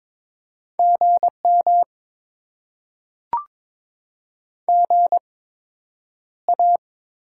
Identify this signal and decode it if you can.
Morse code, heard as a keyed tone, about 22 wpm, spaced as for 9 wpm, 700 Hz. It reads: GM GA
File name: Top 90 QSO Elements - 8x Character Spacing - 22wpm